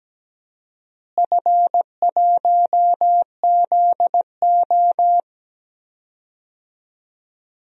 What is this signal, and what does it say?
Morse code, heard as a keyed tone, about 17 wpm, 700 Hz. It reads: F1ZO
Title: DXCC-100 Call Signs - 4 Characters - Repeated 2x - 17wpm